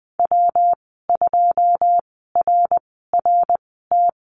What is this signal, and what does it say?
Morse code, heard as a keyed tone, about 20 wpm, spaced as for 14 wpm, 700 Hz. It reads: W2RRT
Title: 161 Lesson 26g - Letter g - Call Signs 20wpm